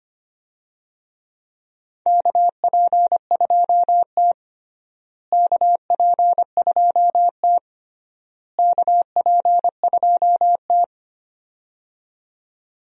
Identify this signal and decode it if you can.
Morse code, heard as a keyed tone, about 25 wpm, 700 Hz. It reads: KP2T KP2T KP2T
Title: USA Call Signs - 4 Characters - Repeated 3x - 25wpm